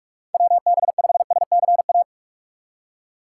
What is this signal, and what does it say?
Morse code, heard as a keyed tone, about 46 wpm, 700 Hz. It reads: WB5SCA
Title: USA Call Signs - 6 Characters - 3x Character Spacing - 45wpm